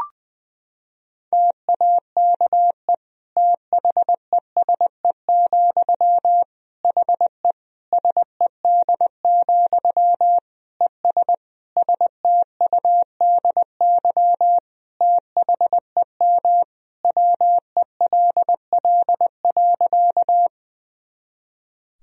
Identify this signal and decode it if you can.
Morse code, heard as a keyed tone, about 20 wpm, 700 Hz. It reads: TAKE THESE, HE SED, ES STUDY THEM WELL.
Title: Sentences from Top 300 Words and Abbreviations 20wpm